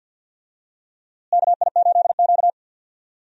Morse code, heard as a keyed tone, about 47 wpm, 690 Hz. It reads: KI8Q
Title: USA Call Signs - 4 Characters - Repeated 2x - 50wpm